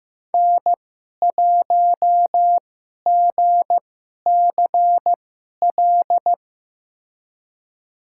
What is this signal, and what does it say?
Morse code, heard as a keyed tone, about 15 wpm, spaced as for 11 wpm, 700 Hz. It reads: N1GCL
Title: USA Call Signs - 2x Character Spacing - 15wpm